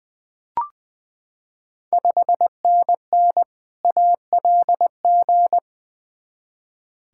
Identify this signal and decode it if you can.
Morse code, heard as a keyed tone, about 20 wpm, 700 Hz. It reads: 5NN ALG
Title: Ontario QSO Party Exchange 20wpm